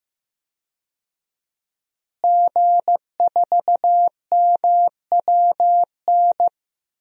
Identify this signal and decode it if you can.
Morse code, heard as a keyed tone, about 15 wpm, 700 Hz. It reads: G4MWN
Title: DXCC-100 Call Signs - 5 Characters - Repeated 3x - 15wpm